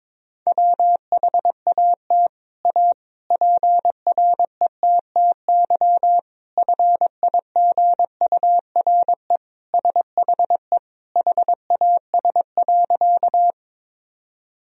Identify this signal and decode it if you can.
Morse code, heard as a keyed tone, about 21 wpm, 700 Hz. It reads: WHAT A PRETTY FIGURE SHE HAS.
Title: Sentences from Top 1000 Words - Review Part 2 - 22wpm